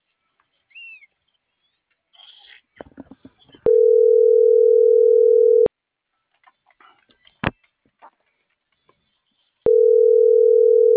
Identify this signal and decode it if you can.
Ringback tone (North America), 440 + 480 Hz.